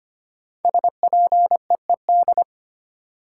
Morse code, heard as a keyed tone, about 25 wpm, 700 Hz. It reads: SPEED
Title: Sentences from Top 800 Words - New Word Drill - 25wpm